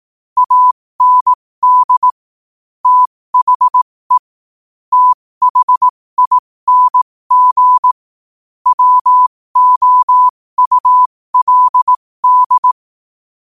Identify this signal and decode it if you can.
Morse code, heard as a keyed tone, about 18 wpm, spaced as for 15 wpm, 1000 Hz. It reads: AND THE THING WOULD
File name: qotdfiles / 15wpm